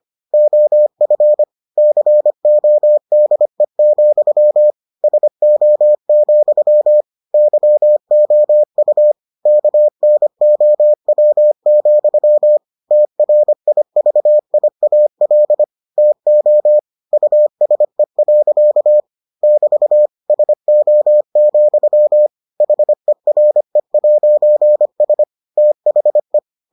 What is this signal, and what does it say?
Morse code, heard as a keyed tone, about 25 wpm, 600 Hz. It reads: OF CODE, SO, YOU KNOW, TRIVIAL TO USE. <BT> SO, HERE'S THE